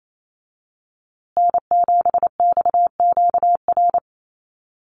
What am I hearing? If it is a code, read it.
Morse code, heard as a keyed tone, about 28 wpm, 700 Hz. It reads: N7XQR
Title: USA Call Signs Repeated 2x 28wpm